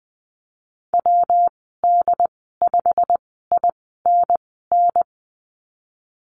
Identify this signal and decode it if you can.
Morse code, heard as a keyed tone, about 20 wpm, spaced as for 15 wpm, 700 Hz. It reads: WD5INN
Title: 128 Lesson 21g - Number 6 - Call Signs 20wpm